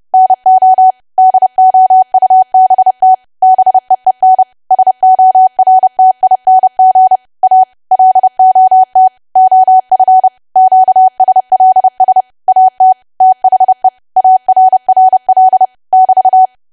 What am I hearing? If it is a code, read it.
Morse code, heard as a keyed tone, about 30 wpm, 750 Hz. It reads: NO DOUBT BEEN SORTING A LOT OF QSLS AT THE ARRL <BT>
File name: cwa / W1AW-Level3 / 30WPM